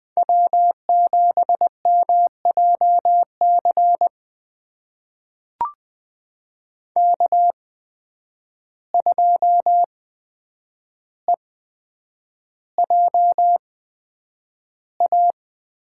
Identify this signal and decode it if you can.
Morse code, heard as a keyed tone, about 20 wpm, spaced as for 9 wpm, 700 Hz. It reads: W7MJC K2EJA
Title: USA Call Signs - 5 Characters - 8x Character Spacing - 20wpm